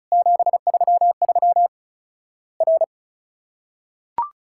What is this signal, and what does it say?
Morse code, heard as a keyed tone, about 36 wpm, 700 Hz. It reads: 733 R
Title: Worked All Europe DX Contest - QTC Exchange - 35wpm